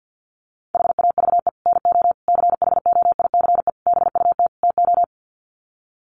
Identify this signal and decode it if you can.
Morse code, heard as a keyed tone, about 50 wpm, 700 Hz. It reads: HAVE NO CHOICE BUT TO
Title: Sets of 5 Words Encore 50wpm